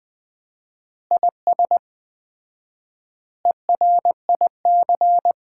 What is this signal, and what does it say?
Morse code, heard as a keyed tone, about 20 wpm, 700 Hz. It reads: IS ERIC
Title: Common QSO Phrases - 4x Word-Spacing - 20wpm